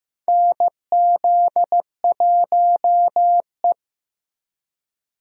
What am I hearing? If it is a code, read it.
Morse code, heard as a keyed tone, about 15 wpm, 700 Hz. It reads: NZ1E
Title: USA Call Signs - 4 Characters - Repeated 2x - 15wpm